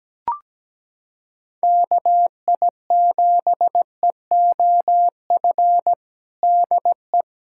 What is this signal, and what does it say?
Morse code, heard as a keyed tone, about 17 wpm, 700 Hz. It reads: KI7EOF DE
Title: USA Callsign DE Callsign - 6 Characters - 17wpm